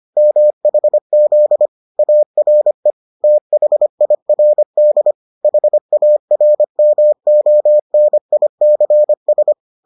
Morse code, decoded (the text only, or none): MHZ ARE THIRD HARMONICS